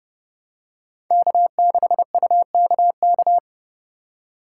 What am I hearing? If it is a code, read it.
Morse code, heard as a keyed tone, about 30 wpm, 700 Hz. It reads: K6UKK
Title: USA Call Signs - 5 Characters - 3x Character Spacing - 30wpm